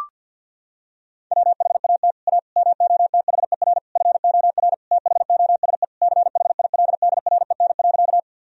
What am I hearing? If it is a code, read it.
Morse code, heard as a keyed tone, about 49 wpm, 700 Hz. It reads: WHAT A MOTHER FOR THOSE CHILDREN.
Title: Sentences from Top 1000 Words - Review Part 1 - 50wpm